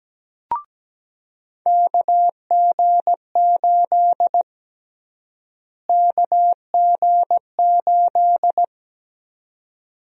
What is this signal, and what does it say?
Morse code, heard as a keyed tone, about 17 wpm, 700 Hz. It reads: KG8 KG8
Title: USA Call Sign Prefixes - Repeated 2x - 17wpm